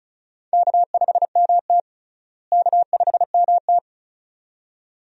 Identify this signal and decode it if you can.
Morse code, heard as a keyed tone, about 36 wpm, 700 Hz. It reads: K5MT K5MT
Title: USA Call Signs - 4 Characters - Repeated 2x - 35wpm